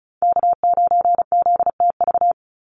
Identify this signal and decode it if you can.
Morse code, heard as a keyed tone, about 34 wpm, 700 Hz. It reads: K9ZTV
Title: Call Signs Repeated 2x - 35wpm